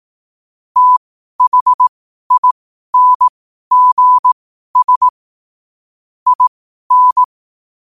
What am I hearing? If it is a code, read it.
Morse code, heard as a keyed tone, about 18 wpm, spaced as for 12 wpm, 1000 Hz. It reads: THINGS IN
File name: qotdfiles / 12wpm